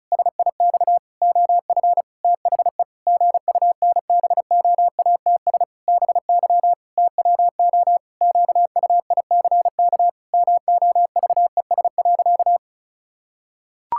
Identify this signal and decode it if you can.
Morse code, heard as a keyed tone, about 36 wpm, 700 Hz. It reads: SIX OF THE GUNBOATS BY TWO QUICK MOVES.
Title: Pangram Challenge v2 - 35wpm